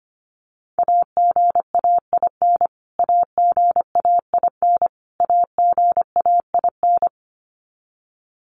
Morse code, heard as a keyed tone, about 25 wpm, 700 Hz. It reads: AGAIN AGAIN AGAIN
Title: Top 500 Words Repeated 25wpm